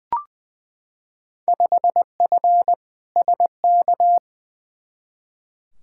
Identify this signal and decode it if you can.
Morse code, heard as a keyed tone, about 20 wpm, 700 Hz. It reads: 5F SK
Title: Field Day Exchanges Encore - 20wpm